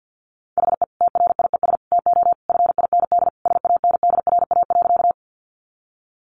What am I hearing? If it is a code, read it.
Morse code, heard as a keyed tone, about 48 wpm, 700 Hz. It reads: HE TRIES TO FIND SANDRA.
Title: Easy Phrases 50wpm